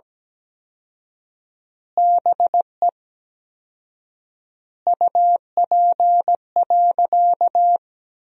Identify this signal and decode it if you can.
Morse code, heard as a keyed tone, about 17 wpm, 700 Hz. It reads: BE UP.
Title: 4x Word-Spacing - Sentences from Top 300 Words - 17wpm